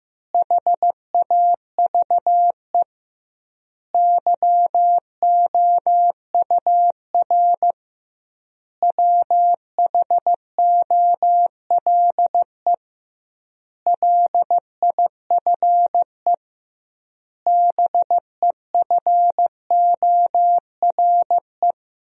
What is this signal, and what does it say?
Morse code, heard as a keyed tone, about 15 wpm, 700 Hz. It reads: HAVE YOUR WHOLE LIFE BEFORE
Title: 2x Word-Spacing - Sentences from Top 400 Words - 15wpm